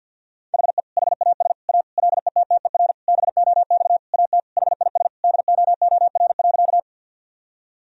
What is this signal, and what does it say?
Morse code, heard as a keyed tone, about 48 wpm, 700 Hz. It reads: HE HAS A LETTER BOX AT HIS DOOR.
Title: Sentences from Top 1000 Words - Review Part 2 - 50wpm